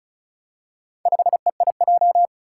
Morse code, heard as a keyed tone, about 36 wpm, 700 Hz. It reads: 5EIJ
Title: Four Challenging Letter-Number Combinations 35wpm